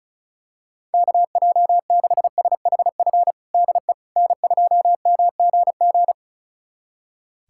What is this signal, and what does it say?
Morse code, heard as a keyed tone, about 36 wpm, 700 Hz. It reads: KJ6SHF DE N2MGG
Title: USA Callsign DE Callsign 35wpm